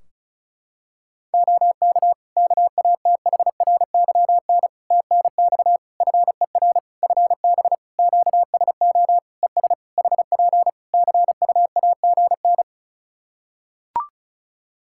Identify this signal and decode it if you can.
Morse code, heard as a keyed tone, about 34 wpm, 700 Hz. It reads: OK KATHRYN TNX FER FB QSO ES HP CUAGN
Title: K1USN SST QSO Practice 35wpm